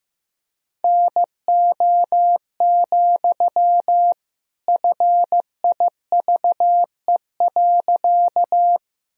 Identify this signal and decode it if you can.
Morse code, heard as a keyed tone, about 16 wpm, 700 Hz. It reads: NO, FIVE.
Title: Sentences from Top 1000 Words - Review Part 2 - 15wpm